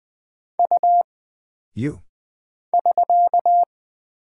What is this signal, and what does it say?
Morse code, heard as a keyed tone, about 20 wpm, 700 Hz. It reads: U <SK>
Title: Single Character (Letter:Number:Punctuation:Prosign) - Mind-Melt - 20wpm